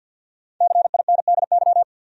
Morse code, heard as a keyed tone, about 47 wpm, 700 Hz. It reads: CINDY
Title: USA Common Names - Max 5 Letters - 50wpm